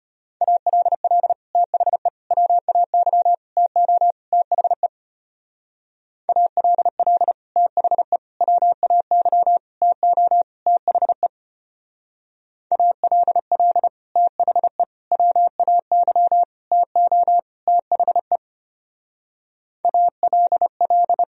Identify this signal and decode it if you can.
Morse code, heard as a keyed tone, about 31 wpm, 700 Hz. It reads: ALL THE WAY TO THE ALL THE WAY TO THE ALL THE WAY TO THE ALL
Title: Speed-Racing - Sets of 5 Words 25wpm